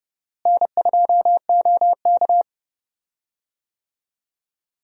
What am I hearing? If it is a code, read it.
Morse code, heard as a keyed tone, about 30 wpm, 700 Hz. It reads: N2OK
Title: USA Call Signs - 4 Characters - Repeated 3x - 30wpm